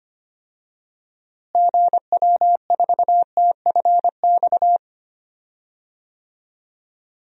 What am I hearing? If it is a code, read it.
Morse code, heard as a keyed tone, about 25 wpm, 700 Hz. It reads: GW4TFX
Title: DXCC-100 Call Signs - 6 Characters - Repeated 2x - 25wpm